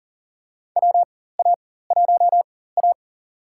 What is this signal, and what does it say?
Morse code, heard as a keyed tone, about 40 wpm, spaced as for 19 wpm, 700 Hz. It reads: WA1A
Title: USA Call Signs - 4 Characters - 4x Character Spacing - 40wpm